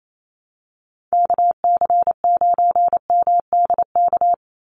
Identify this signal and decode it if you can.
Morse code, heard as a keyed tone, about 28 wpm, 700 Hz. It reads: KC9MDK
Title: USA Call Signs 28wpm